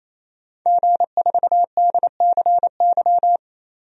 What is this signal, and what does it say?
Morse code, heard as a keyed tone, about 28 wpm, 700 Hz. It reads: G4DCY
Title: DXCC-100 Call Signs - 5 Characters - 28wpm